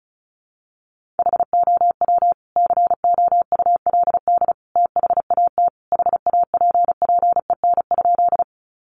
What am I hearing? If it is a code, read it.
Morse code, heard as a keyed tone, about 34 wpm, 700 Hz. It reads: HOW COULD THAT HAPPEN?
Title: Sentences from Top 400 Words 35wpm